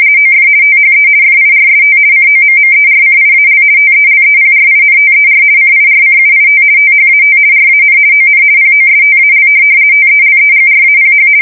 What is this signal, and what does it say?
RTTY, about 50 baud. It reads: OF THE FIRST COMMERCIAL TELEPRINTER CIRCUIT THAT RAN ON POSTAL TELEGRAPH
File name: fsk170-50B/teletype-corp